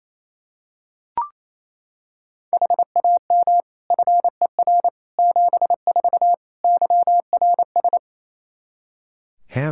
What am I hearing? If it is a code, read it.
Morse code, heard as a keyed tone, about 28 wpm, 700 Hz. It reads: HAM FER 74 YRS
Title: Practice QSO - Exchange 3 - More Info - 28wpm